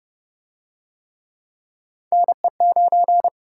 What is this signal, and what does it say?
Morse code, heard as a keyed tone, about 29 wpm, 700 Hz. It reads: NE9